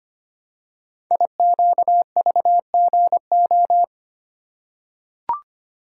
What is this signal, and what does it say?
Morse code, heard as a keyed tone, about 25 wpm, 700 Hz. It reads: IQVGO
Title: Code Groups - 5 Letters - 25wpm